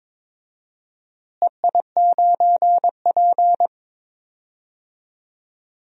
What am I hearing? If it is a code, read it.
Morse code, heard as a keyed tone, about 22 wpm, 700 Hz. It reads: EI9P
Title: DXCC-100 Call Signs - 4 Characters - Repeated 3x - 22wpm